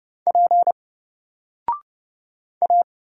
Morse code, heard as a keyed tone, about 30 wpm, 700 Hz. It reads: P A
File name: Single-Letter-Number-30wpm